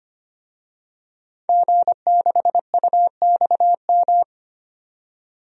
Morse code, heard as a keyed tone, about 25 wpm, 700 Hz. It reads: G6UXM
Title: DXCC-100 Call Signs - 5 Characters - 25wpm